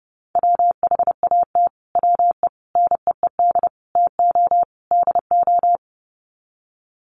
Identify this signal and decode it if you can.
Morse code, heard as a keyed tone, about 28 wpm, 700 Hz. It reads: WHAT WE NEED TO DO
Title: Sets of 5 Words Encore 30wpm